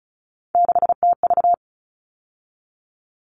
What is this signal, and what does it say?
Morse code, heard as a keyed tone, about 34 wpm, 700 Hz. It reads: 6TV
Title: All Cut Numbers - 3-Digit 35wpm